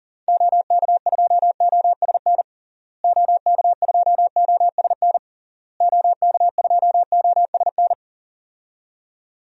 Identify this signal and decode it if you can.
Morse code, heard as a keyed tone, about 40 wpm, 700 Hz. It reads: OK2OSN OK2OSN OK2OSN
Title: DXCC-100 Call Signs - 6 Characters - Repeated 3x - 40wpm